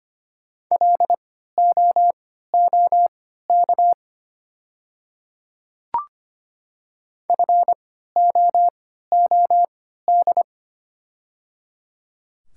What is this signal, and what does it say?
Morse code, heard as a keyed tone, about 25 wpm, spaced as for 14 wpm, 700 Hz. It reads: LOOK FOOD
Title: 3x Character Spacing - Top 300 Words - 25wpm